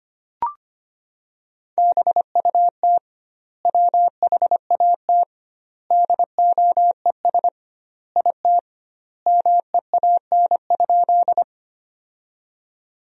Morse code, heard as a keyed tone, about 25 wpm, 700 Hz. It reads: BUT WHAT DOES IT MEAN?
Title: 2x Word-Spacing - Sentences from Top 200 Words - 25wpm